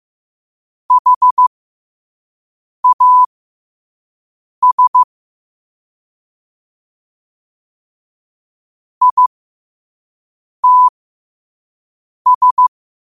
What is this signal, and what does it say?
Morse code, heard as a keyed tone, about 15 wpm, spaced as for 5 wpm, 1000 Hz. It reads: HAS ITS
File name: qotdfiles / 5wpm